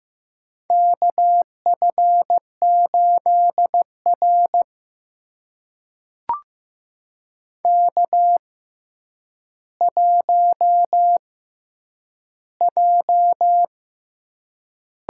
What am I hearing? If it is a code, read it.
Morse code, heard as a keyed tone, about 15 wpm, spaced as for 9 wpm, 700 Hz. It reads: KF8R K1J
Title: USA Call Signs - 6x Character Spacing - 15wpm